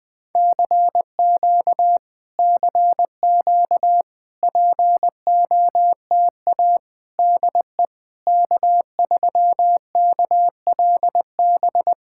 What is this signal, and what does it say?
Morse code, heard as a keyed tone, about 20 wpm, 700 Hz. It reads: CQ CQ POTA DE K3KLB
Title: POTA QSO Practice - 20wpm